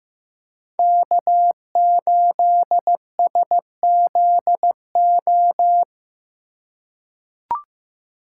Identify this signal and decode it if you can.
Morse code, heard as a keyed tone, about 15 wpm, 700 Hz. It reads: K8SZO
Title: USA Call Signs - 5 Characters - 4x Character Spacing - 15wpm